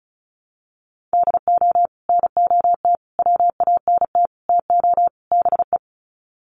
Morse code, heard as a keyed tone, about 34 wpm, 700 Hz. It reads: DO NOT WANT TO BE
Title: Sets of 5 Words 35wpm